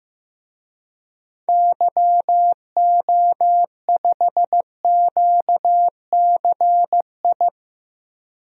Morse code, heard as a keyed tone, about 15 wpm, 700 Hz. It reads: YO5QCI
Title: DXCC-100 Call Signs - 6 Characters - 15wpm